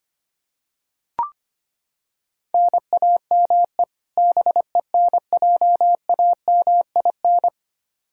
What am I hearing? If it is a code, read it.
Morse code, heard as a keyed tone, about 25 wpm, 700 Hz. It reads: NAME BENJAMIN
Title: K1USN SST QSO Practice 25wpm